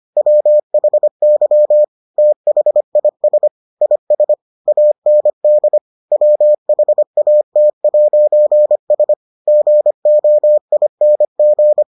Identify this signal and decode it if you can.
Morse code, heard as a keyed tone, about 25 wpm, 600 Hz. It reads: WHY THIS IS AND WHAT'S GOING